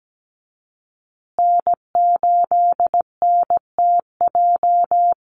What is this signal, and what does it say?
Morse code, heard as a keyed tone, about 17 wpm, 700 Hz. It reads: N8NTJ
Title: USA Call Signs Repeated 3x 17wpm